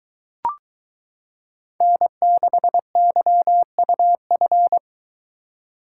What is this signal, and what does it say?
Morse code, heard as a keyed tone, about 22 wpm, 700 Hz. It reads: N6YUF